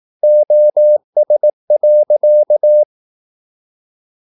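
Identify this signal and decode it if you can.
Morse code, heard as a keyed tone, about 18 wpm, 600 Hz. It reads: OS.
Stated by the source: Slow (QRS) Morse practice at 17 wpm, from CW med Gnister.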